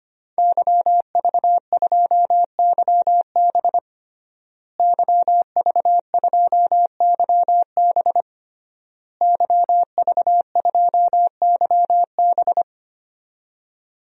Morse code, heard as a keyed tone, about 25 wpm, 700 Hz. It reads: YV2YB YV2YB YV2YB
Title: DXCC-100 Call Signs - 5 Characters - Repeated 3x - 25wpm